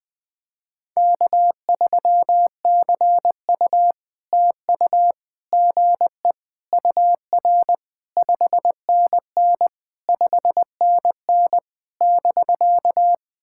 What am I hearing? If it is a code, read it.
Morse code, heard as a keyed tone, about 20 wpm, 690 Hz. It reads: K3CU TU GE UR 5NN 5NN <BK>
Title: POTA QSO Practice - 20wpm